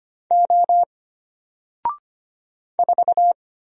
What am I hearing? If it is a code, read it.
Morse code, heard as a keyed tone, about 25 wpm, 700 Hz. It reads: O 4
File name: Single-Letter-Number-25wpm